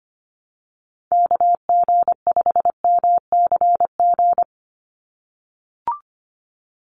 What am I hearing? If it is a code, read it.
Morse code, heard as a keyed tone, about 25 wpm, 700 Hz. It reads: KG5MCG
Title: USA Call Signs 25wpm